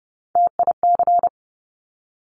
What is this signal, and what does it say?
Morse code, heard as a keyed tone, about 30 wpm, 700 Hz. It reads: TIC
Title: Top Three Letter Combinations in Words 30wpm